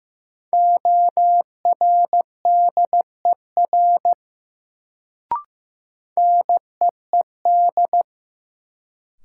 Morse code, heard as a keyed tone, about 15 wpm, 700 Hz. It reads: ORDER NEED
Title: Sets of 2 Words - New Word Drill - 15wpm